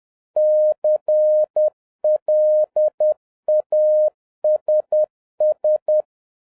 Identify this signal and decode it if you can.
Morse code, heard as a keyed tone, about 10 wpm, 610 Hz. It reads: CLASS